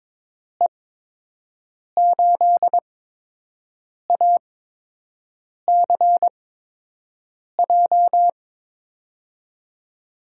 Morse code, heard as a keyed tone, about 22 wpm, spaced as for 6 wpm, 700 Hz. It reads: E8ACJ
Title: USA Call Signs - 6 Characters - 8x Character Spacing - 22wpm